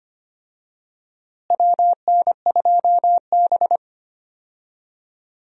Morse code, heard as a keyed tone, about 25 wpm, 700 Hz. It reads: WN2B
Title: USA Call Signs - 4 Characters - Repeated 3x - 25wpm